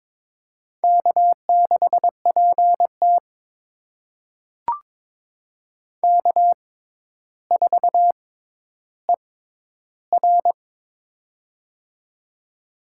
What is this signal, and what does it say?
Morse code, heard as a keyed tone, about 22 wpm, spaced as for 11 wpm, 700 Hz. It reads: K6PT K4ER
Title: USA Call Signs - 4 Characters - 6x Character Spacing - 22wpm